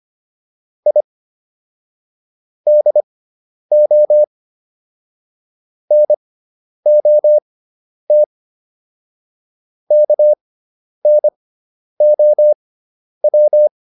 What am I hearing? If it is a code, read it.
Morse code, heard as a keyed tone, about 25 wpm, spaced as for 10 wpm, 600 Hz. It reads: I DO NOT KNOW